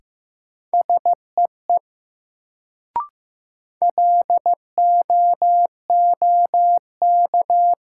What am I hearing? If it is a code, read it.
Morse code, heard as a keyed tone, about 16 wpm, 700 Hz. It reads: SEE LOOK
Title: Sets of 2 Words - New Word Drill - 15wpm